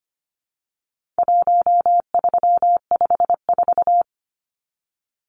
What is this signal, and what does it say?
Morse code, heard as a keyed tone, about 25 wpm, 700 Hz. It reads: 1354
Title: Numbers - 4-Digit 25wpm